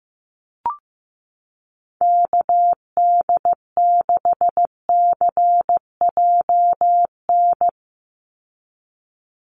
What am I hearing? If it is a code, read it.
Morse code, heard as a keyed tone, about 15 wpm, 700 Hz. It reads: KD6CJN
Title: USA Call Signs 15wpm